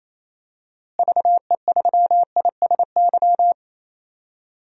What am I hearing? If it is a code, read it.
Morse code, heard as a keyed tone, about 28 wpm, 700 Hz. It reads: VE3ISY
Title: DXCC-100 Call Signs - 6 Characters - Repeated 2x - 28wpm